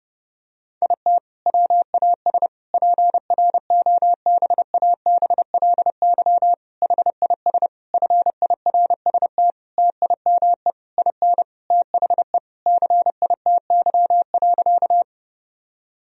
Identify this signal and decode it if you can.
Morse code, heard as a keyed tone, about 30 wpm, 700 Hz. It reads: IT WAS PROBABLY HIS FIRST TIME IN THE CITY.
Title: Sentences from Top 1000 Words - Review Part 4 - 30wpm